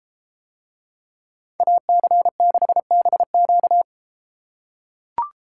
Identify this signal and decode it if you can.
Morse code, heard as a keyed tone, about 31 wpm, 700 Hz. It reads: AC6BQ